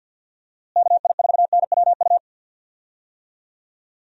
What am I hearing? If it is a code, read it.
Morse code, heard as a keyed tone, about 51 wpm, 700 Hz. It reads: KI4NWU
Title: USA Call Signs - 6 Characters - Repeated 3x - 50wpm